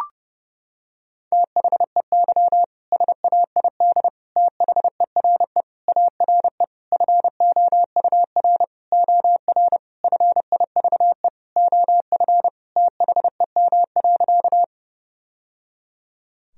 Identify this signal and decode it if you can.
Morse code, heard as a keyed tone, about 30 wpm, 700 Hz. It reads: THEY SAID THERE ARE FOUR OR FIVE OF THEM.
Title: Sentences from Top 1000 Words - Review Part 2 - 30wpm